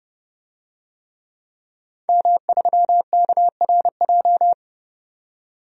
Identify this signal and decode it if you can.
Morse code, heard as a keyed tone, about 30 wpm, 700 Hz. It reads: M3KRJ
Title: DXCC-100 Call Signs - 5 Characters - 30wpm